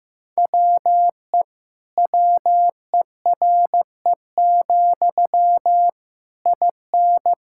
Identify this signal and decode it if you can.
Morse code, heard as a keyed tone, about 16 wpm, 700 Hz. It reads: WE WERE, IN
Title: Sentences from Top 1000 Words - Review Part 5 - 15wpm